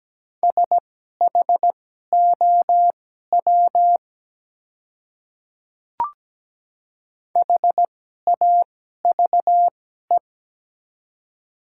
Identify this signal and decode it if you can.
Morse code, heard as a keyed tone, about 17 wpm, spaced as for 12 wpm, 700 Hz. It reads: SHOW HAVE
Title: 2x Character Spacing - Top 200 Words - 17wpm